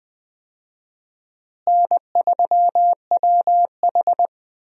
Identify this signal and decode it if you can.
Morse code, heard as a keyed tone, about 20 wpm, 700 Hz. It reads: N3WH
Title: USA Call Signs - 4 Characters - Repeated 3x - 20wpm